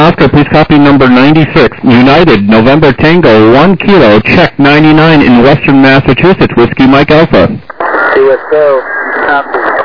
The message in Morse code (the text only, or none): none